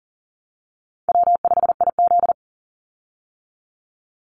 Morse code, heard as a keyed tone, about 40 wpm, 700 Hz. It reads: W5IZ
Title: Easy Call Signs Repeated 3x - 40wpm